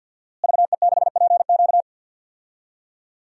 Morse code, heard as a keyed tone, about 46 wpm, 700 Hz. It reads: VE6PQ
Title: DXCC-100 Call Signs - 5 Characters - Repeated 3x - 50wpm